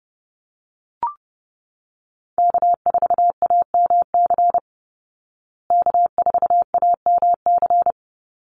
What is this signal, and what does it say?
Morse code, heard as a keyed tone, about 30 wpm, 700 Hz. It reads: K4AMC K4AMC
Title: Call Signs Repeated 2x - 30wpm